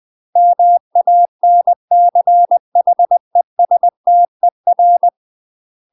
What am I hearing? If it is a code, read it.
Morse code, heard as a keyed tone, about 20 wpm, 700 Hz. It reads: MANCHESTER
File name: UK MASTER QSOs TRAINING / 20WPM / UK MASTER QSOs 0005